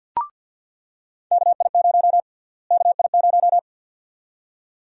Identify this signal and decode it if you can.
Morse code, heard as a keyed tone, about 48 wpm, 700 Hz. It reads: KI0 KI0
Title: USA Call Sign Prefixes - Repeated 2x - 50wpm